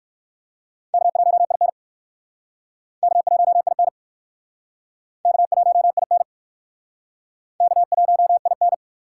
Morse code, heard as a keyed tone, about 56 wpm, 700 Hz. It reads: K1IN K1IN K1IN K1IN
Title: Speed Racing - USA 1x2 Call Signs 45wpm